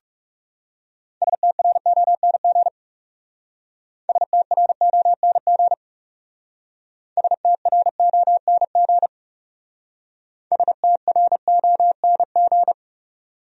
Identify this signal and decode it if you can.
Morse code, heard as a keyed tone, about 36 wpm, 700 Hz. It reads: STRONG STRONG STRONG STRONG
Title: Speed-Racing Top 500 Words 30wpm